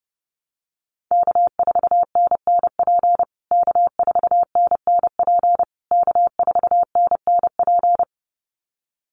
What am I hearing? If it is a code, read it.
Morse code, heard as a keyed tone, about 30 wpm, 700 Hz. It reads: K4NNP K4NNP K4NNP
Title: USA Call Signs Repeated 3x 30wpm